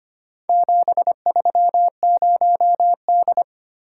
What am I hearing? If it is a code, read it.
Morse code, heard as a keyed tone, about 25 wpm, 700 Hz. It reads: 730D
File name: practice / 25wpm / Four Letter-Number Combinations - 25wpm